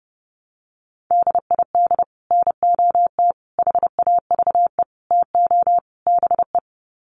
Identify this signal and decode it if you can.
Morse code, heard as a keyed tone, about 30 wpm, 700 Hz. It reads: DID NOT HAVE TO BE
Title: Sets of 5 Words Encore2 30wpm